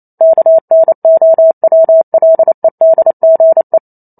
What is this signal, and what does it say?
Morse code, heard as a keyed tone, about 28 wpm, 650 Hz. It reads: KNOWLEDGE